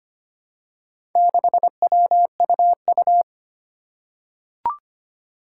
Morse code, heard as a keyed tone, about 25 wpm, 700 Hz. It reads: 6WUU
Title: Four Challenging Letter-Number Combinations 25wpm